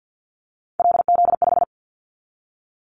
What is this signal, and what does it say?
Morse code, heard as a keyed tone, about 50 wpm, 690 Hz. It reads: LZ5
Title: Callsign Prefixes for Top 100 DXCC - 50wpm